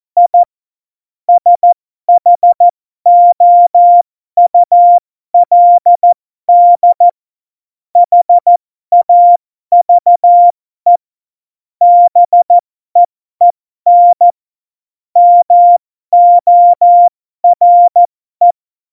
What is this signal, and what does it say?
Morse code, heard as a keyed tone, about 14 wpm, spaced as for 12 wpm, 700 Hz. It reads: I SHOULD HAVE BEEN MORE
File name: Morsekurs / Witze